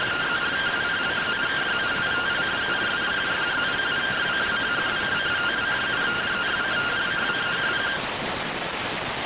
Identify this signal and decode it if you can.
Two-tone teleprinter signal, about 170 Hz shift.